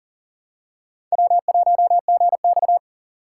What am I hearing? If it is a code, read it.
Morse code, heard as a keyed tone, about 40 wpm, 700 Hz. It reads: W1GX
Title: USA Call Signs - 8x Character Spacing - 40wpm